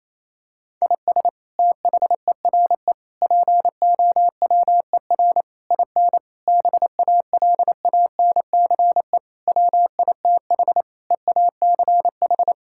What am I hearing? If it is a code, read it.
Morse code, heard as a keyed tone, about 28 wpm, 700 Hz. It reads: IS THERE POWER IN BALANCE WITH EACH
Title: Sentences from Top 1000 Words - Review Part 6 - 28wpm